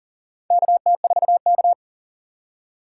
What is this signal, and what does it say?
Morse code, heard as a keyed tone, about 40 wpm, 700 Hz. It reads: KT4K
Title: USA Call Signs - 4x Character Spacing - 40wpm